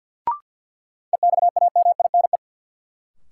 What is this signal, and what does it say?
Morse code, heard as a keyed tone, about 47 wpm, 700 Hz. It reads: EXAMINE
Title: Sentences from Top 1600 Words - New Word Drill Alt - 50wpm